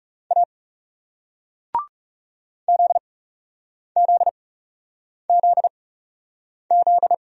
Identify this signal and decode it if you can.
Morse code, heard as a keyed tone, about 41 wpm, spaced as for 8 wpm, 700 Hz. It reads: A ZZZZ